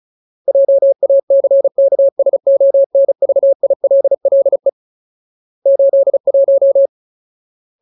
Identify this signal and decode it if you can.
Morse code, heard as a keyed tone, about 34 wpm, 550 Hz. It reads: JACKSONVILLE 81
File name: Ultimate Word List / 35WPM Ultimate